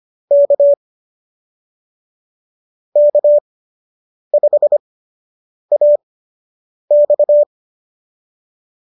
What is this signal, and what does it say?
Morse code, heard as a keyed tone, about 25 wpm, spaced as for 8 wpm, 590 Hz. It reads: K K5AX